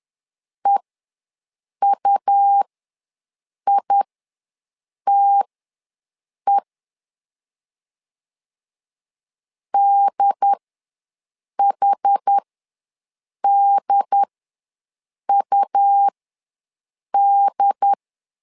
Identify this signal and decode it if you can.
Morse code, heard as a keyed tone, about 10 wpm, spaced as for 6 wpm, 800 Hz. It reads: EUITE DHDUD
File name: LEZIONI / 2 UHDN / 2 UHD ws 10wpm 2058